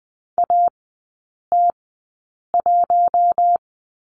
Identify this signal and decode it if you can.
Morse code, heard as a keyed tone, about 20 wpm, spaced as for 8 wpm, 700 Hz. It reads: AT1
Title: 053 Lesson 10e - Letter r - Three Characters 20wpm